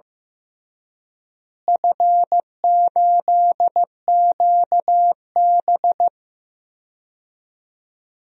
Morse code, heard as a keyed tone, about 15 wpm, 700 Hz. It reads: F8QB
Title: DXCC-100 Call Signs - 4 Characters - Repeated 3x - 15wpm